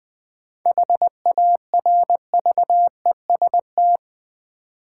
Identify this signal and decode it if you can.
Morse code, heard as a keyed tone, about 20 wpm, 700 Hz. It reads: HARVEST
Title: 7 Letter Words - First 3 Letters Given - 20wpm